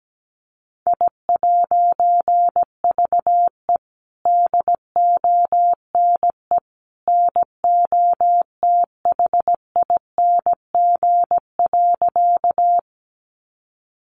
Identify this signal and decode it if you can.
Morse code, heard as a keyed tone, about 17 wpm, 700 Hz. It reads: I'VE DONE NOTHING.
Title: A Sound Of Thunder 17wpm